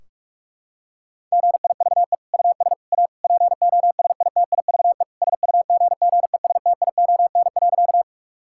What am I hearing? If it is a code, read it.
Morse code, heard as a keyed tone, about 46 wpm, 700 Hz. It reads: GIVE US A POSITIVE SUGGESTION.
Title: Sentences from Top 1600 Words 45wpm